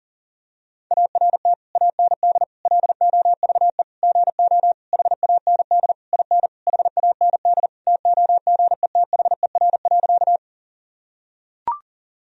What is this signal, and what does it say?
Morse code, heard as a keyed tone, about 40 wpm, 700 Hz. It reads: ART AND LOVE GO HAND IN HAND TOGETHER.
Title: Sentences from Top 1000 Words - Review Part 3 - 40wpm